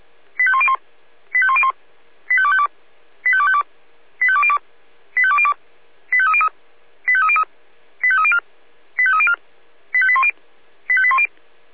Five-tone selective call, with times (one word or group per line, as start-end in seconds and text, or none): 0.4-0.8 85101
1.3-1.7 85101
2.3-2.7 85252
3.2-3.6 85252
4.2-4.6 85202
5.2-5.5 85202
6.1-6.5 85303
7.1-7.4 85303
8.0-8.4 85304
9.0-9.3 85304
9.9-10.3 85810
10.9-11.3 85810